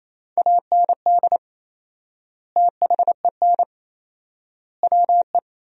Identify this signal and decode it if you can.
Morse code, heard as a keyed tone, about 28 wpm, 700 Hz. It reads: AND THEN WE
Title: 4x Word-Spacing - Sentences from Top 300 Words - 28wpm